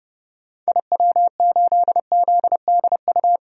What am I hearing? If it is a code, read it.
Morse code, heard as a keyed tone, about 30 wpm, 690 Hz. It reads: IW8ZDU
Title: DXCC-100 Call Signs - 6 Characters - 30wpm